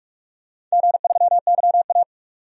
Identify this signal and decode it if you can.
Morse code, heard as a keyed tone, about 44 wpm, 700 Hz. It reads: G3YA
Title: DXCC-100 Call Signs - 4 Characters - 45wpm